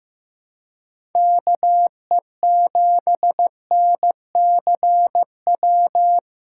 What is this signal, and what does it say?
Morse code, heard as a keyed tone, about 15 wpm, 700 Hz. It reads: KE7NCW
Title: USA Call Signs - 6 Characters - 4x Character Spacing - 15wpm